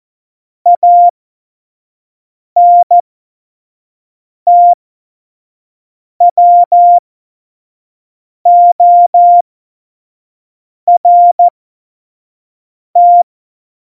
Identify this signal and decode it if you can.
Morse code, heard as a keyed tone, about 14 wpm, spaced as for 5 wpm, 700 Hz. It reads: ANTWORT
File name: Morsekurs / Witze / Witz 08. Januar 2019, 25 F70